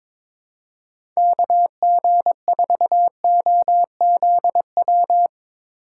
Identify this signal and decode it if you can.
Morse code, heard as a keyed tone, about 22 wpm, 700 Hz. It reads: KG4OZW
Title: USA Call Signs - 6 Characters - 4x Character Spacing - 22wpm